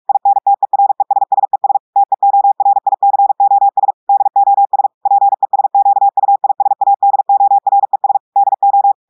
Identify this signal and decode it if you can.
Morse code, heard as a keyed tone, about 46 wpm, 800 Hz. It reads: INTERESSES TEORICOS DOS PESQUISADORES DO
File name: portugais texte en clair 1 vitesse 45 mots minutes